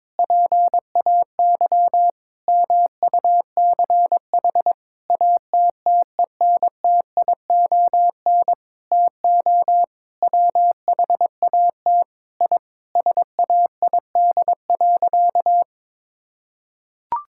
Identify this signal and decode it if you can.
Morse code, heard as a keyed tone, about 22 wpm, 700 Hz. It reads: PAY MUCH ATTENTION TO WHAT I SAID.
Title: Sentences from Top 1000 Words - Review Part 6 - 22wpm